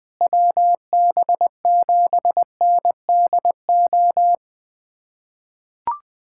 Morse code, heard as a keyed tone, about 21 wpm, 700 Hz. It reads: WB7NDO
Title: USA Callsign DE Callsign 20wpm